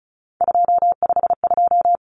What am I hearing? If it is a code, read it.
Morse code, heard as a keyed tone, about 34 wpm, 700 Hz. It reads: 252
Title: Common Cut Numbers - 3-Digit 35wpm